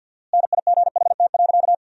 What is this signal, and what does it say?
Morse code, heard as a keyed tone, about 50 wpm, 700 Hz. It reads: NIGHT.
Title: 4x Word-Spacing - Sentences from Top 300 Words - 50wpm